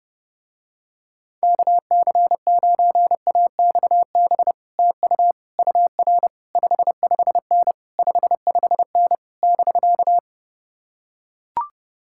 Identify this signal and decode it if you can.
Morse code, heard as a keyed tone, about 31 wpm, 700 Hz. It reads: KC9AXB TU UR 55N 55N <BK>